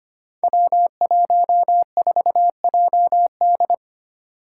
Morse code, heard as a keyed tone, about 25 wpm, 700 Hz. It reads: W14JD
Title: Five Challenging Letter-Number Combinations 25wpm